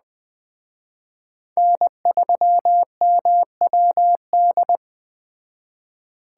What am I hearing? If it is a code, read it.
Morse code, heard as a keyed tone, about 20 wpm, 700 Hz. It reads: N3MWD
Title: Speed Racing - USA 1x3 Call Signs 17wpm